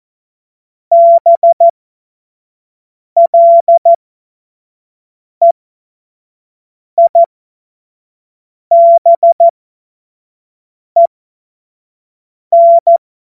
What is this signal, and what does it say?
Morse code, heard as a keyed tone, about 14 wpm, spaced as for 5 wpm, 680 Hz. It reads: BLEIBEN
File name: Witz-29-Juli-2025-25-F70